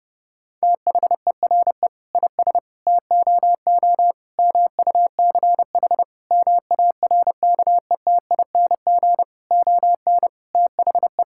Morse code, heard as a keyed tone, about 30 wpm, 700 Hz. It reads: THERE IS TOO MUCH MARKETING ON THE
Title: Sentences from Top 1000 Words - Review Part 5 - 30wpm